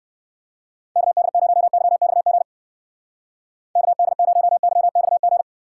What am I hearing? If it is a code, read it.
Morse code, heard as a keyed tone, about 55 wpm, 700 Hz. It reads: KD0YXG KD0YXG
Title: Speed Racing - USA 2x3 Call Signs 45wpm